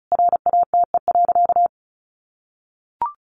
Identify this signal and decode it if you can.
Morse code, heard as a keyed tone, about 34 wpm, 700 Hz. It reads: RATE.
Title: Sentences from Top 900 Words 35wpm